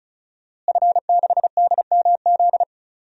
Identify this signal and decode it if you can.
Morse code, heard as a keyed tone, about 34 wpm, 700 Hz. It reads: F6DMZ
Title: DXCC-100 Call Signs - 5 Characters - Repeated 2x - 35wpm